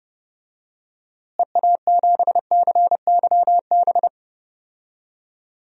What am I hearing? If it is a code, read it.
Morse code, heard as a keyed tone, about 32 wpm, 700 Hz. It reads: EA7CYB
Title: DXCC-100 Call Signs - 6 Characters - Repeated 3x - 30wpm